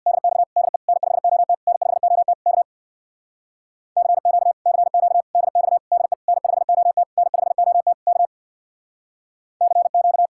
Morse code, heard as a keyed tone, about 64 wpm, 700 Hz. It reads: DX DE N5OT N5OT K CQ CQ DX DE N5OT N5OT K CQ